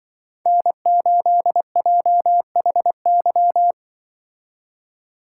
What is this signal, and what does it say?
Morse code, heard as a keyed tone, about 24 wpm, 700 Hz. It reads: N8JHY
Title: Speed Racing - USA 1x3 Call Signs 20wpm